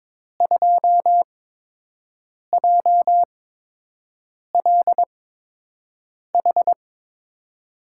Morse code, heard as a keyed tone, about 22 wpm, spaced as for 6 wpm, 700 Hz. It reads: 2JLH
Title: USA Call Signs - 5 Characters - 8x Character Spacing - 22wpm